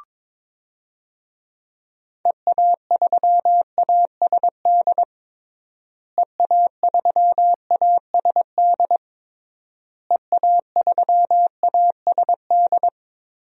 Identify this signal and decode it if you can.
Morse code, heard as a keyed tone, about 22 wpm, 700 Hz. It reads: EA3ASD EA3ASD EA3ASD
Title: DXCC-100 Call Signs - 6 Characters - Repeated 3x - 22wpm